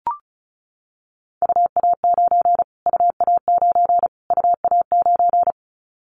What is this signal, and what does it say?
Morse code, heard as a keyed tone, about 34 wpm, 700 Hz. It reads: UA9 UA9 UA9
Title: Callsign Prefixes for Top 25 DXCC - Repeated 3x - 35wpm